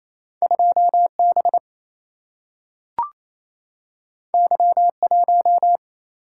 Morse code, heard as a keyed tone, about 28 wpm, 700 Hz. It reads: 2B Y1
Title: Single-Letter Single-Number 28wpm